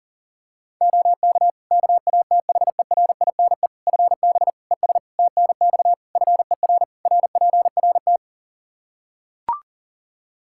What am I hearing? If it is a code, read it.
Morse code, heard as a keyed tone, about 37 wpm, 700 Hz. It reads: OK KATHERINE FB ES TNX FER RPRT
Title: K1USN SST QSO Practice 40wpm